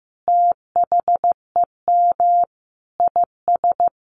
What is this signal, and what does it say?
Morse code, heard as a keyed tone, about 15 wpm, 700 Hz. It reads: THEM IS
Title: Additional Easy Phrases 15wpm